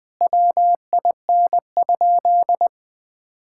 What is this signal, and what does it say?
Morse code, heard as a keyed tone, about 20 wpm, 700 Hz. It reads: WIN?
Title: Sentences with 4-Word Phrases - 20wpm